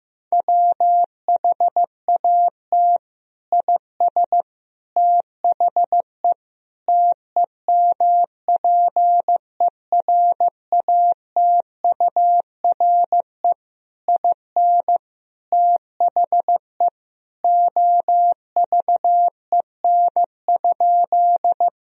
Morse code, heard as a keyed tone, about 15 wpm, 700 Hz. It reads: WHAT IS THE TEMPERATURE IN THE OVEN?
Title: Sentences from Top 1000 Words - Review Part 4 - 15wpm